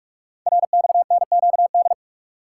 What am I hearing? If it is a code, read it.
Morse code, heard as a keyed tone, about 44 wpm, 700 Hz. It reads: RXNQD
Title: Code Groups - 5 Letters - 45wpm